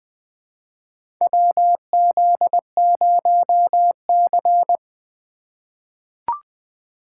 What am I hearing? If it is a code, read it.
Morse code, heard as a keyed tone, about 20 wpm, 700 Hz. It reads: WZ0C
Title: USA Call Signs - 4 Characters - 2x Character Spacing - 20wpm